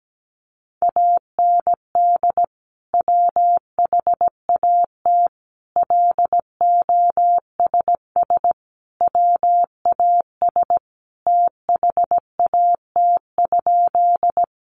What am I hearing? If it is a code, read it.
Morse code, heard as a keyed tone, about 17 wpm, 700 Hz. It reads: AND WHAT LOSS WAS THAT?
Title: Sentences from Top 1400 Words 17wpm